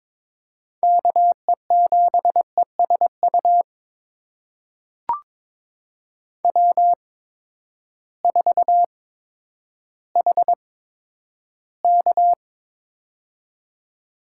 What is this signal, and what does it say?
Morse code, heard as a keyed tone, about 22 wpm, spaced as for 11 wpm, 700 Hz. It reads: KE7ESU W4HK
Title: USA Call Signs - 8x Character Spacing - 22wpm